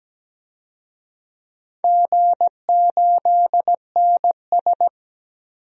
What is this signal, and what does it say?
Morse code, heard as a keyed tone, about 17 wpm, 700 Hz. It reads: G8NS
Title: DXCC-100 Call Signs - 4 Characters - Repeated 2x - 17wpm